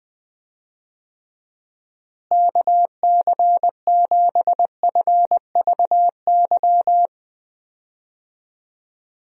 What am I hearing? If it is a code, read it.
Morse code, heard as a keyed tone, about 21 wpm, 700 Hz. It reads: KC7FVY
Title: USA Call Signs - 6 Characters - Repeated 2x - 20wpm